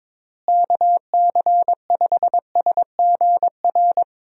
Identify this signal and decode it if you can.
Morse code, heard as a keyed tone, about 22 wpm, 700 Hz. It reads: KC5SGR
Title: USA Call Signs - 6 Characters - Repeated 3x - 22wpm